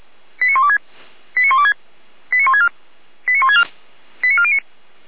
Five-tone selective call (ZVEI), heard as quick runs of five tones, tones of about 70 ms. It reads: 89126 89126 89164 89164 89409